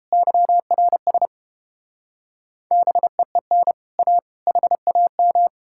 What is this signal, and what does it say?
Morse code, heard as a keyed tone, about 33 wpm, 700 Hz. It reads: YRS BEEN A HAM